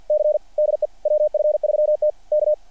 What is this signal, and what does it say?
Morse code, heard as a keyed tone, about 55 wpm, 600 Hz. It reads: C DE WF3T K
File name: archived / CRSnet